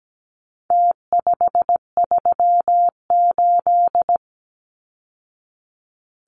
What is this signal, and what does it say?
Morse code, heard as a keyed tone, about 17 wpm, 700 Hz. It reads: T538
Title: Numbers - Contest Style - 4-Digit 17wpm